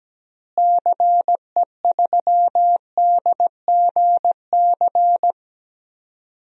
Morse code, heard as a keyed tone, about 17 wpm, 700 Hz. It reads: CE3DGC
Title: DXCC-100 Call Signs - 6 Characters - Repeated 2x - 17wpm